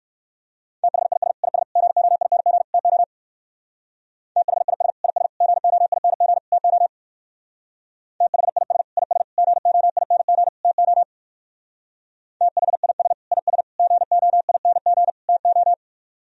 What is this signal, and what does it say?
Morse code, heard as a keyed tone, about 61 wpm, 700 Hz. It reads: THIS IS GOING TO THIS IS GOING TO THIS IS GOING TO THIS IS GOING TO